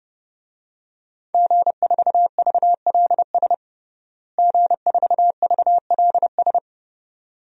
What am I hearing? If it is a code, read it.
Morse code, heard as a keyed tone, about 30 wpm, 700 Hz. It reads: G4VLS G4VLS
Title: DXCC-100 Call Signs - 5 Characters - Repeated 2x - 30wpm